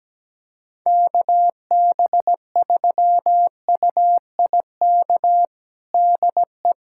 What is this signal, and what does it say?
Morse code, heard as a keyed tone, about 17 wpm, 700 Hz. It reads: KB3UIK DE
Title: USA Callsign DE Callsign - 6 Characters - 17wpm